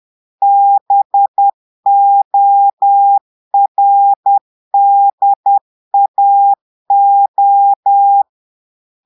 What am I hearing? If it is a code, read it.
Morse code, heard as a keyed tone, about 10 wpm, 800 Hz. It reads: BORDAO
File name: portugais texte en clair 2 vitesse 10 mots minutes